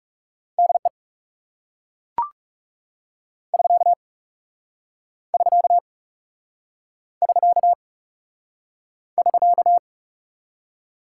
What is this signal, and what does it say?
Morse code, heard as a keyed tone, about 40 wpm, 700 Hz. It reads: DE <SK> <SK> <SK> <SK>